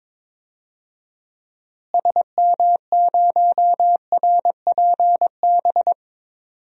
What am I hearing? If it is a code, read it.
Morse code, heard as a keyed tone, about 22 wpm, 700 Hz. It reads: SM0RPB
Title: DXCC-100 Call Signs - 6 Characters - Repeated 3x - 22wpm